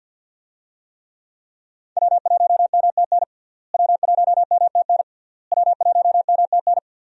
Morse code, heard as a keyed tone, about 46 wpm, 700 Hz. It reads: W1MTN W1MTN W1MTN
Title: USA Call Signs - 5 Characters - Repeated 3x - 50wpm